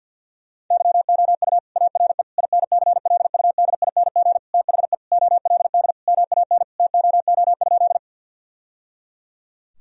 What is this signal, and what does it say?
Morse code, heard as a keyed tone, about 50 wpm, 700 Hz. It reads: YOU ARE INCLUDING THE OLD MAN TOO?
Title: Sentences from Top 1000 Words - Review Part 4 - 50wpm